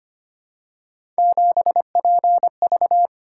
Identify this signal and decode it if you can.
Morse code, heard as a keyed tone, about 25 wpm, 700 Hz. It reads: 7PV
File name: Three Letter-Number Combinations - 25wpm